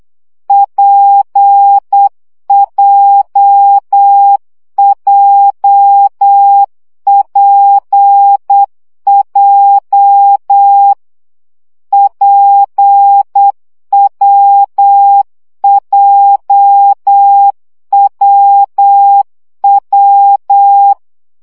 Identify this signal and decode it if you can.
Morse code, heard as a keyed tone, about 8 wpm, 800 Hz. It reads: PJJPJ PWJWW
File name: CW gyakorlas w p j 7 baud